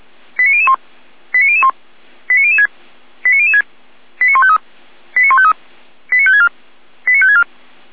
Five-tone selective call (ZVEI), heard as quick runs of five tones, tones of about 70 ms. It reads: 89001 89001 89006 89006 89153 89153 89564 89564
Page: /SDR/ZVEI/